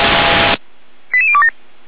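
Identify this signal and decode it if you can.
Five-tone selective call (ZVEI), heard as quick runs of five tones, tones of about 80 ms. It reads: 88027
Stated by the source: ZVEI five-tone selective call 88141, received by SDR for feuerwehr-orlamuende fire service.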